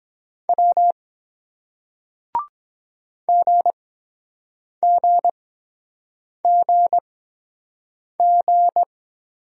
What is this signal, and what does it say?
Morse code, heard as a keyed tone, about 21 wpm, spaced as for 7 wpm, 700 Hz. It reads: W GGGG
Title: Speed-Racing Single Letters 17wpm